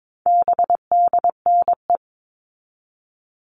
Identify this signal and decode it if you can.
Morse code, heard as a keyed tone, about 22 wpm, 700 Hz. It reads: BDNE
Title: All Cut Numbers - 4-Digit 22wpm